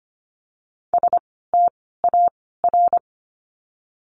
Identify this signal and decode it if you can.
Morse code, heard as a keyed tone, about 25 wpm, 700 Hz. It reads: S T A R
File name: course / 25:10wpm / 094 Lesson 16f - Letter c - Words 25wpm